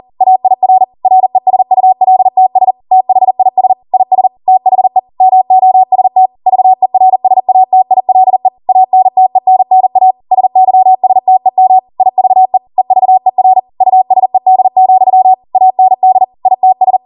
Morse code, heard as a keyed tone, about 40 wpm, 750 Hz. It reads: AIR RESULTS THIS IS THE MOST VERSATILE ANTENNA SYSTEM IVE EVER USED, AND ITS